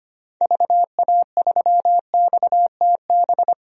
Morse code, heard as a keyed tone, about 25 wpm, 700 Hz. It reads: VA3XTB
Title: DXCC-100 Call Signs - 6 Characters - Repeated 3x - 25wpm